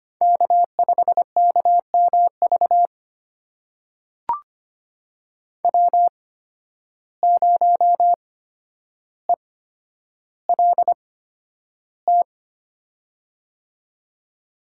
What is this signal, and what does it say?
Morse code, heard as a keyed tone, about 25 wpm, spaced as for 11 wpm, 700 Hz. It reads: K5KMV W0ELT
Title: USA Call Signs - 5 Characters - 8x Character Spacing - 25wpm